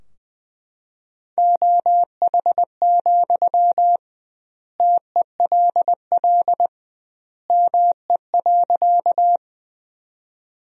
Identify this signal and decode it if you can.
Morse code, heard as a keyed tone, about 20 wpm, 700 Hz. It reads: OH, TELL ME.
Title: 2x Word-Spacing - Sentences from Top 500 Words - 20wpm